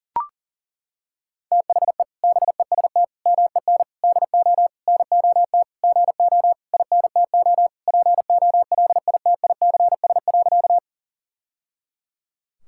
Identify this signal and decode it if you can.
Morse code, heard as a keyed tone, about 40 wpm, 700 Hz. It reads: THE BEST MEN DO NOT GO INTO POLITICS.
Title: Sentences from Top 1000 Words - Review Part 6 - 40wpm